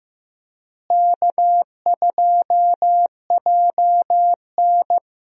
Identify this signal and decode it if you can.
Morse code, heard as a keyed tone, about 15 wpm, 700 Hz. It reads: K2JN
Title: USA Call Signs - 4 Characters - Repeated 2x - 15wpm